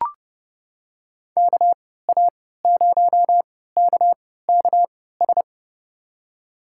Morse code, heard as a keyed tone, about 30 wpm, spaced as for 17 wpm, 700 Hz. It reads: KA0KKS
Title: USA Call Signs - 6 Characters - 3x Character Spacing - 30wpm